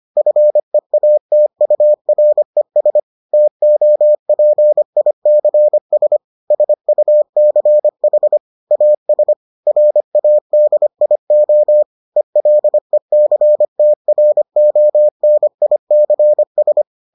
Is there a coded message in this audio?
Yes: Morse code from 0.2 to 16.8 s, heard as a keyed tone, about 25 wpm, 600 Hz. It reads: FEATURES TOPICS SUCH AS RADIO ELECTRONICS